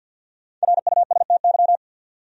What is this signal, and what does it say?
Morse code, heard as a keyed tone, about 49 wpm, 700 Hz. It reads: RUSTY
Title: USA Common Names - 50wpm